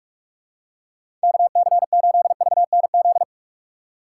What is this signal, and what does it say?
Morse code, heard as a keyed tone, about 46 wpm, 700 Hz. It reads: KC8VNZ
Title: USA Call Signs - 3x Character Spacing - 45wpm